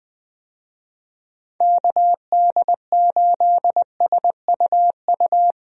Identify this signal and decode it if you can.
Morse code, heard as a keyed tone, about 20 wpm, 700 Hz. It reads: KD8SUU
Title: USA Call Signs - 6 Characters - Repeated 2x - 20wpm